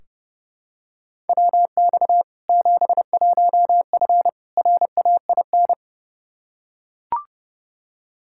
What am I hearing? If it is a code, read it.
Morse code, heard as a keyed tone, about 30 wpm, 700 Hz. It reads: WX 71F RAIN
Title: K1USN SST QSO Practice 30wpm